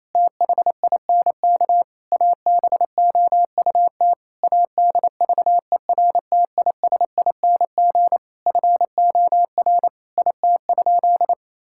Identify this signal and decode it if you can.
Morse code, heard as a keyed tone, about 27 wpm, 700 Hz. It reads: THINK ABOUT ADVERTISING FOR IT?
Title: Sentences from Top 1000 Words - Review Part 6 - 28wpm